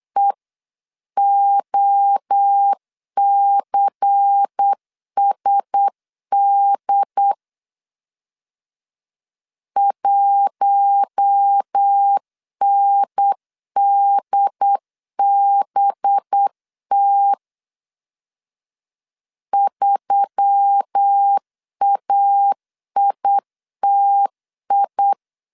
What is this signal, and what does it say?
Morse code, heard as a keyed tone, about 8 wpm, 800 Hz. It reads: EOCSD 1NDBT 3AITI
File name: 8wpm